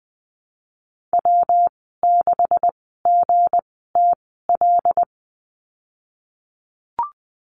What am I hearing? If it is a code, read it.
Morse code, heard as a keyed tone, about 20 wpm, spaced as for 14 wpm, 700 Hz. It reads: W6GTL
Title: 173 Lesson 28g - Number 9 - Call Signs 20wpm